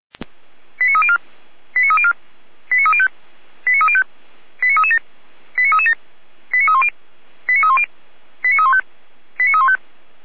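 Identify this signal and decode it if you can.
Five-tone selective call (ZVEI), heard as quick runs of five tones, tones of about 80 ms. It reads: 89304 89304 89305 89305 89337 89337 89310 89310 89315 89315
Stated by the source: ZVEI five-tone call received via SDR, recorded by feuerwehr-orlamuende fire brigade (f85360).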